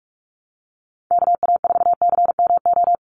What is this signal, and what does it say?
Morse code, heard as a keyed tone, about 46 wpm, 700 Hz. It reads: KA4CMO
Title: USA Call Signs 45wpm